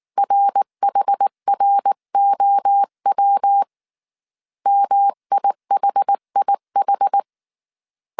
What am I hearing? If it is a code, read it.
Morse code, heard as a keyed tone, about 19 wpm, 800 Hz. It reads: RHROW MIHIH